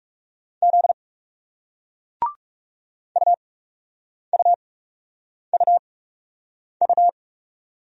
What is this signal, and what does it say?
Morse code, heard as a keyed tone, about 39 wpm, spaced as for 8 wpm, 700 Hz. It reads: Z UUUU